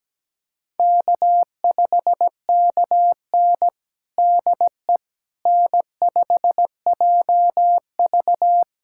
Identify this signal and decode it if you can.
Morse code, heard as a keyed tone, about 17 wpm, 700 Hz. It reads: K5KN DE N5JV
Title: USA Callsign DE Callsign - 4 Characters - 17wpm